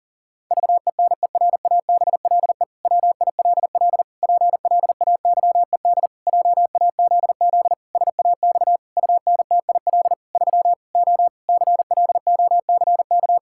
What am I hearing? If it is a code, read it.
Morse code, heard as a keyed tone, about 40 wpm, 700 Hz. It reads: VENERABLE WILL PLAYED JAZZ SAX UNTIL 3 O CLOCK
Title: Pangram Challenge v2 - 40wpm